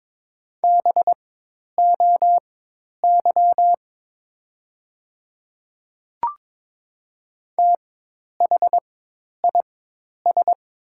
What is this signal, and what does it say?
Morse code, heard as a keyed tone, about 22 wpm, spaced as for 10 wpm, 700 Hz. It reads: BOY THIS
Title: 4x Character Spacing - Top 200 Words - 22wpm